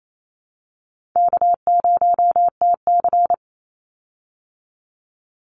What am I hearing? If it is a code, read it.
Morse code, heard as a keyed tone, about 28 wpm, 700 Hz. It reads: K0TC
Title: Call Signs Repeated 3x - 28wpm